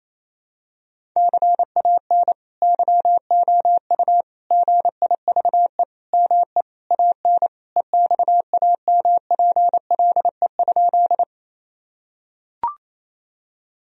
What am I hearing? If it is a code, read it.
Morse code, heard as a keyed tone, about 28 wpm, 700 Hz. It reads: CAN YOU GIVE ME AN EXAMPLE?
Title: Sentences from Top 1000 Words - Review Part 1 - 28wpm